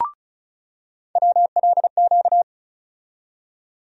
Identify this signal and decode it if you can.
Morse code, heard as a keyed tone, about 34 wpm, 700 Hz. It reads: WLQ
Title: USA Call Sign Postfixes - 3 Letters 35wpm